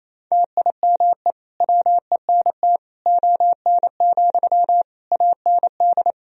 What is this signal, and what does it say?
Morse code, heard as a keyed tone, about 27 wpm, 700 Hz. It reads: TIME WENT ON, AND
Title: Sentences from Top 1000 Words - Review Part 3 - 28wpm